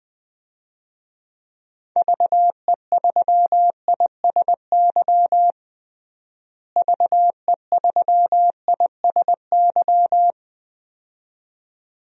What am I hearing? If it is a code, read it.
Morse code, heard as a keyed tone, about 20 wpm, 700 Hz. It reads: VE3ISY VE3ISY
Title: DXCC-100 Call Signs - 6 Characters - Repeated 2x - 20wpm